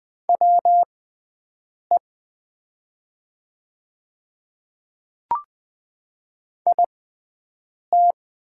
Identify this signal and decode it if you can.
Morse code, heard as a keyed tone, about 20 wpm, spaced as for 7 wpm, 700 Hz. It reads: WE IT
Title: 6x Character Spacing - Top 100 Words - 20wpm